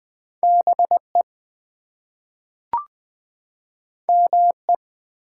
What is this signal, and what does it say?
Morse code, heard as a keyed tone, about 20 wpm, 700 Hz. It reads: BE ME
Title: Two Letter Words - 20wpm